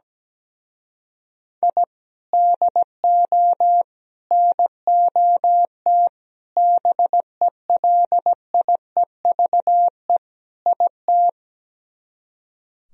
Morse code, heard as a keyed tone, about 17 wpm, 700 Hz. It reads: I DO NOT BELIEVE IT
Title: Speed-Racing - Sets of 5 Words 17wpm